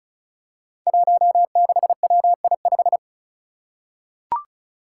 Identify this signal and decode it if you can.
Morse code, heard as a keyed tone, about 36 wpm, 700 Hz. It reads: 16WI5
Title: Five Challenging Letter-Number Combinations 35wpm